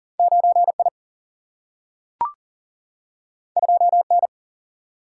Morse code, heard as a keyed tone, about 40 wpm, 700 Hz. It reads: 9I 2N
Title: Single-Letter Single-Number 40wpm